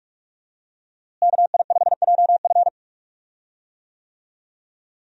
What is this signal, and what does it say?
Morse code, heard as a keyed tone, about 46 wpm, 700 Hz. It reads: KI5JF
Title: USA Call Signs - 5 Characters - Repeated 3x - 45wpm